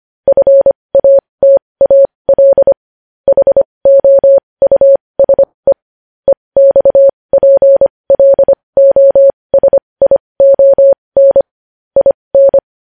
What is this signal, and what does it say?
Morse code, heard as a keyed tone, about 25 wpm, spaced as for 20 wpm, 550 Hz. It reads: FATAL HOUSE EXPLOSION IN
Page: news-headlines-in-morse-code-at-20-wpm